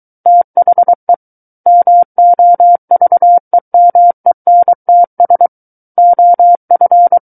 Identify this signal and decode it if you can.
Morse code, heard as a keyed tone, about 23 wpm, 700 Hz. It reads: THE MOVEMENTS OF
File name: Book One / 14-Chapter 14 - 23 WPM